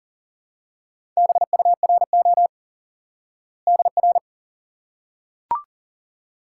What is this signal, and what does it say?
Morse code, heard as a keyed tone, about 40 wpm, 700 Hz. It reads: BURO DR
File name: IQE Course 3 - Lesson 89 - YAGI - Two Elements - 40wpm